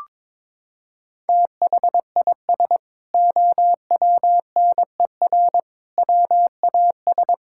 Morse code, heard as a keyed tone, about 22 wpm, 700 Hz. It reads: THIS OWNER WAS
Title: Sentences from Top 1600 Words 22wpm